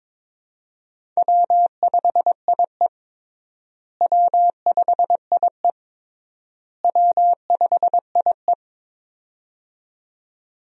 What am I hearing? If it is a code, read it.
Morse code, heard as a keyed tone, about 22 wpm, 700 Hz. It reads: W5IE W5IE W5IE
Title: USA Call Signs - 4 Characters - Repeated 3x - 22wpm